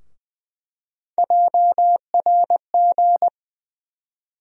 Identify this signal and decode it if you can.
Morse code, heard as a keyed tone, about 20 wpm, 700 Hz. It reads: JRG
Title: USA Call Sign Postfixes - 3 Letters 20wpm